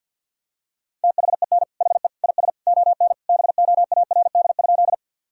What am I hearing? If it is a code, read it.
Morse code, heard as a keyed tone, about 50 wpm, 700 Hz. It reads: THEN HE IS ON BOARD?
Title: Sentences from Top 1000 Words - Review Part 4 - 50wpm